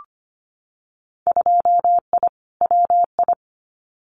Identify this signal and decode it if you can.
Morse code, heard as a keyed tone, about 25 wpm, 700 Hz. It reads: 2I WI
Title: Winter Field Day Exchanges 25wpm